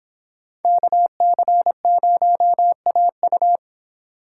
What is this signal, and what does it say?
Morse code, heard as a keyed tone, about 26 wpm, 700 Hz. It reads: KC0AU